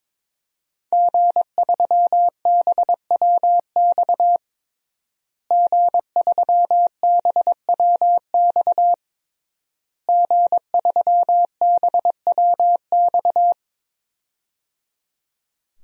Morse code, heard as a keyed tone, about 22 wpm, 700 Hz. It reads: G3BWX G3BWX G3BWX
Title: DXCC-100 Call Signs - 5 Characters - Repeated 3x - 22wpm